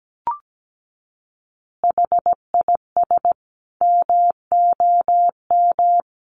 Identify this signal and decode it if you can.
Morse code, heard as a keyed tone, about 17 wpm, 700 Hz. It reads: HIS MOM
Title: Additional Easy Phrases 17wpm